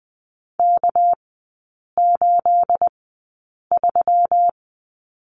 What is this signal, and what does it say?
Morse code, heard as a keyed tone, about 20 wpm, spaced as for 8 wpm, 700 Hz. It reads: K83
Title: 212 Lesson 34e - Number 8 - Three Characters 20wpm